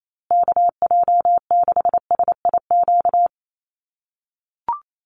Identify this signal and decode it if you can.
Morse code, heard as a keyed tone, about 28 wpm, 700 Hz. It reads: KJ6SIQ
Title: USA Call Signs Repeated 2x 28wpm